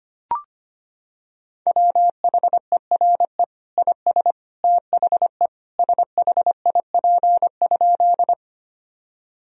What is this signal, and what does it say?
Morse code, heard as a keyed tone, about 25 wpm, 700 Hz. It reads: WHERE IS THE SHIP?
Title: Sentences from Top 1000 Words - Review Part 2 - 25wpm